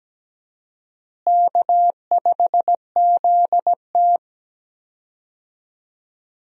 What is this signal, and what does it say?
Morse code, heard as a keyed tone, about 17 wpm, 700 Hz. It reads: K5ZT
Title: USA Call Signs - 4 Characters - Repeated 2x - 17wpm